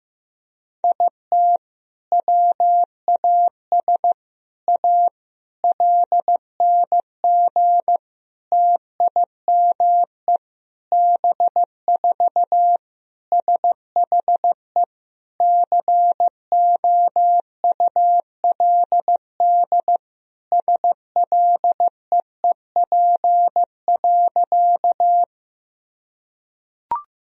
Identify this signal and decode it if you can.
Morse code, heard as a keyed tone, about 15 wpm, 700 Hz. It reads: IT WAS A LNG TIME B4 SHE COULD SLEEP.
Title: Sentences from Top 500 Words and Abbreviations 15wpm